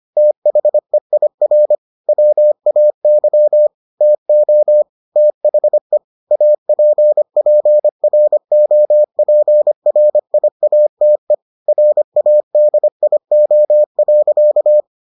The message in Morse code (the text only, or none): THEIR WAY TO THE APPROPRIATE RADIO.